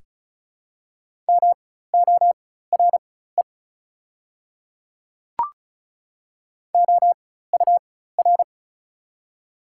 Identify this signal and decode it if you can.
Morse code, heard as a keyed tone, about 34 wpm, spaced as for 16 wpm, 690 Hz. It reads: MORE OUR
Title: 4x Character Spacing - Top 200 Words - 35wpm